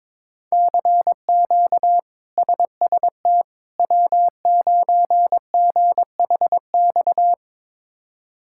Morse code, heard as a keyed tone, about 22 wpm, 700 Hz. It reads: CQ SST W9GHX
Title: K1USN SST QSO Practice - 22wpm